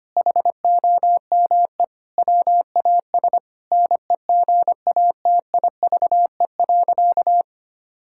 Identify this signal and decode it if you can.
Morse code, heard as a keyed tone, about 25 wpm, 700 Hz. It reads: HOME WAS NEGATIVE.
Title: Sentences from Top 1000 Words - Review Part 6 - 25wpm